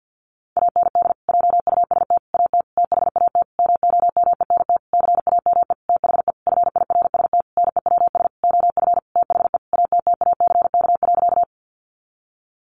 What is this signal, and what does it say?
Morse code, heard as a keyed tone, about 50 wpm, 700 Hz. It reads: AND JUST AT THAT MOMENT CAME THE FIRST NEWS OF THE ATTACK.
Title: Sentences from Top 1500 Words 50wpm